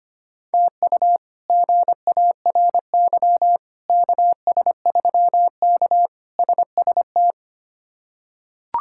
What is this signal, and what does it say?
Morse code, heard as a keyed tone, about 25 wpm, 700 Hz. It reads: TU GARY KS3K SST
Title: K1USN SST QSO Practice - 25wpm